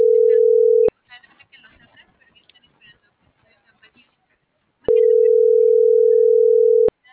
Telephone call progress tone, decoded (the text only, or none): ringback tone